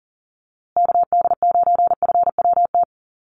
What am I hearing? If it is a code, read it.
Morse code, heard as a keyed tone, about 40 wpm, 700 Hz. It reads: KD9FWT
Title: USA Call Signs Repeated 2x 40wpm